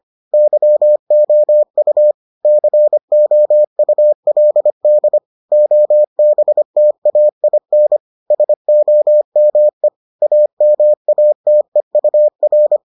Morse code, heard as a keyed tone, about 25 wpm, 600 Hz. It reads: YOU COULD OBTAIN SOME AMATEUR